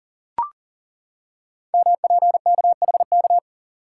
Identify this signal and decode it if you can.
Morse code, heard as a keyed tone, about 42 wpm, 690 Hz. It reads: MPKHK
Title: Code Groups - 5 Letter:Numbers - 40wpm